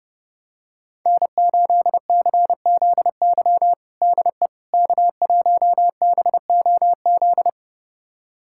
Morse code, heard as a keyed tone, about 30 wpm, 700 Hz. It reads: N8CZY DE K1BOZ
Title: USA Callsign DE Callsign - 5 Characters - 30wpm